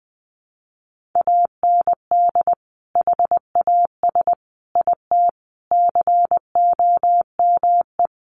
Morse code, heard as a keyed tone, about 20 wpm, 700 Hz. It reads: AND HAS IT COME
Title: Sentences from Top 100 Words 20wpm